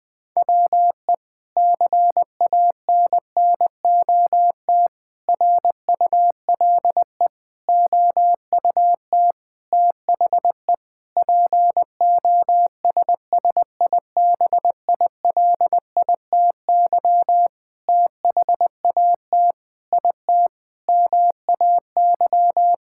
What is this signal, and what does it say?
Morse code, heard as a keyed tone, about 20 wpm, 700 Hz. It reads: WE CANNOT RULE OUT THE POSSIBILITY THAT IT MAY
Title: Sentences with 8-Word Phrases - 20wpm